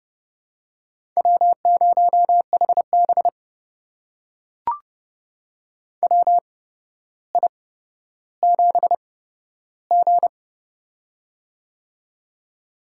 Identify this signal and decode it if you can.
Morse code, heard as a keyed tone, about 30 wpm, spaced as for 13 wpm, 700 Hz. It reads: W0HB WI7G
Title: USA Call Signs - 4 Characters - 8x Character Spacing - 30wpm